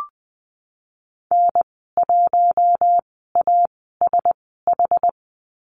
Morse code, heard as a keyed tone, about 19 wpm, spaced as for 14 wpm, 700 Hz. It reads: N1ASH
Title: 082 Lesson 14g - Number 2 - Call Signs 20wpm